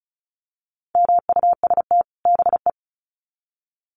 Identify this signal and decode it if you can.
Morse code, heard as a keyed tone, about 34 wpm, 700 Hz. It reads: MUST BE
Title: Sets of 2 Words 35wpm